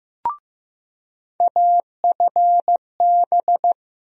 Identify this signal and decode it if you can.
Morse code, heard as a keyed tone, about 15 wpm, 700 Hz. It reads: AFB
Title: USA Call Sign Postfixes - 3 Letters 15wpm